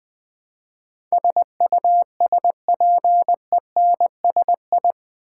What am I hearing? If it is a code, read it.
Morse code, heard as a keyed tone, about 20 wpm, 700 Hz. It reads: SUSPENSI
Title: Build A Word - 4 Letters At A Time - 10-Letter Words - 20wpm